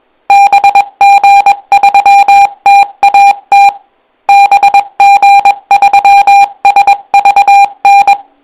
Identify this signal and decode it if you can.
Morse code, heard as a keyed tone, about 22 wpm, 800 Hz. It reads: BG3TAT BG3SVN